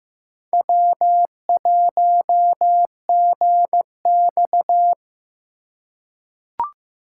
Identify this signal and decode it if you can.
Morse code, heard as a keyed tone, about 15 wpm, 700 Hz. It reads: W1GX
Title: USA Call Signs - 4 Characters - 2x Character Spacing - 15wpm